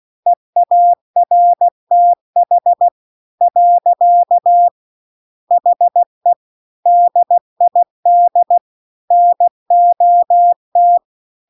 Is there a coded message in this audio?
Yes: Morse code from 0.3 to 11.0 s, heard as a keyed tone, about 16 wpm, 700 Hz. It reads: EARTH . HE DID NOT